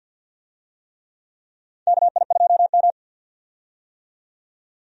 Morse code, heard as a keyed tone, about 49 wpm, 700 Hz. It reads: KI2M
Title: USA Call Signs - 4 Characters - 50wpm